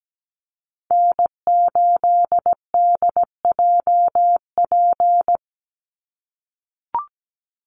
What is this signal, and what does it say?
Morse code, heard as a keyed tone, about 17 wpm, 700 Hz. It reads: N8DJP
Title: USA Call Signs 17wpm